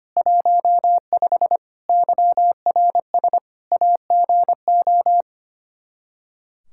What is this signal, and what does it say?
Morse code, heard as a keyed tone, about 25 wpm, 700 Hz. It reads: 15 YRS AGO
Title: K1USN SST QSO Practice 25wpm